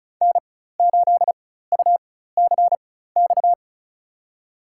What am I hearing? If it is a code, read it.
Morse code, heard as a keyed tone, about 36 wpm, spaced as for 16 wpm, 700 Hz. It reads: N8UCX
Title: USA Call Signs - 5 Characters - 4x Character Spacing - 35wpm